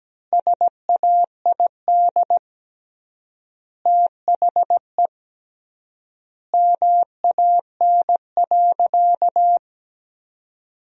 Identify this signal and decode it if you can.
Morse code, heard as a keyed tone, about 17 wpm, 700 Hz. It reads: SAID THE MAN.
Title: 3x Word-Spacing - Sentences from Top 400 Words - 17wpm